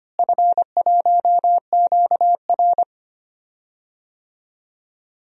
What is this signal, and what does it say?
Morse code, heard as a keyed tone, about 25 wpm, 700 Hz. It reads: F1QR
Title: DXCC-100 Call Signs - 4 Characters - Repeated 3x - 25wpm